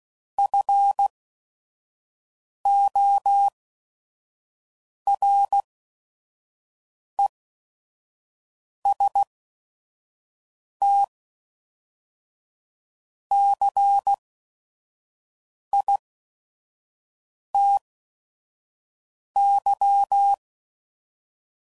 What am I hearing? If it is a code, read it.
Morse code, heard as a keyed tone, about 16 wpm, spaced as for 5 wpm, 800 Hz. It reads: FORESTCITY
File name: CW Code Groups @5WPM / fromSM's / Cities